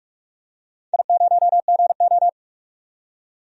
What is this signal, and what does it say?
Morse code, heard as a keyed tone, about 46 wpm, 700 Hz. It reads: I0GO
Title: DXCC-100 Call Signs - 4 Characters - 45wpm